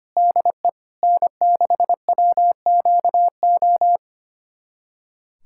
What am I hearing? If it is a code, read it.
Morse code, heard as a keyed tone, about 25 wpm, 700 Hz. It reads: DE N6WQO
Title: USA Callsign DE Callsign 25wpm